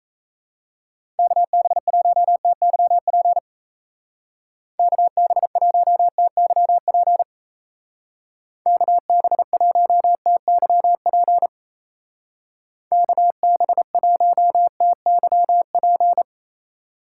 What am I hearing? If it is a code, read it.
Morse code, heard as a keyed tone, about 34 wpm, 700 Hz. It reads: KB1TYP KB1TYP KB1TYP KB1TYP
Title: Speed Racing - USA 2x3 Call Signs 28wpm